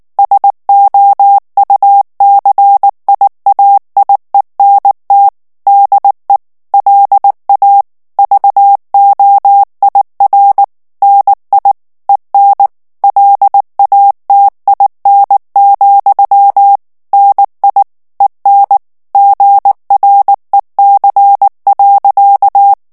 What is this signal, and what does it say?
Morse code, heard as a keyed tone, about 19 wpm, 800 Hz. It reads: SOUCIAIENT DE LA VOIR NI EN LATIN, NI EN GREC.
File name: CW / FICHIERS-MP3 / FRANCAIS TEXTE CLAIR 2 VITESSE 18 MOTS MINUTES